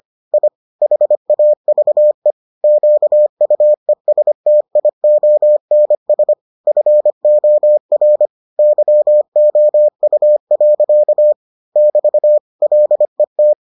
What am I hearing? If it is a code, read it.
Morse code, heard as a keyed tone, about 25 wpm, 600 Hz. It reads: I HAVE QUESTIONS FOR YOU. <BT> LET